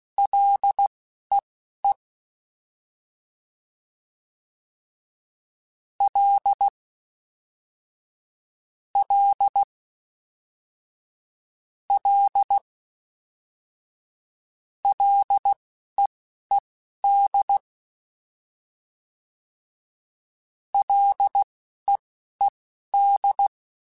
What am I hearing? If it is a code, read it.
Morse code, heard as a keyed tone, about 16 wpm, spaced as for 11 wpm, 800 Hz. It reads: LEE L L L LEED LEED